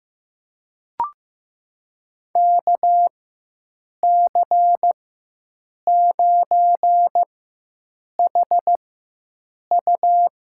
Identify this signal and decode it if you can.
Morse code, heard as a keyed tone, about 15 wpm, spaced as for 7 wpm, 700 Hz. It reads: KC9HU
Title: USA Call Signs - 6 Characters - 4x Character Spacing - 15wpm